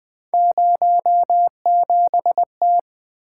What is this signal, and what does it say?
Morse code, heard as a keyed tone, about 20 wpm, 700 Hz. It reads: 07T
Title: Three Letter-Number Combinations - 20wpm